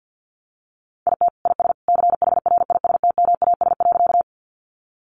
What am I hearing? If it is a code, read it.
Morse code, heard as a keyed tone, about 50 wpm, 700 Hz. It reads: IT IS CHRISTMAS.
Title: Easy Phrases 50wpm